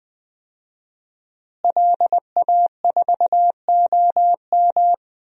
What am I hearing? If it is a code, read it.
Morse code, heard as a keyed tone, about 20 wpm, 700 Hz. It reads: LA4OM
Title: DXCC-100 Call Signs - 5 Characters - Repeated 2x - 20wpm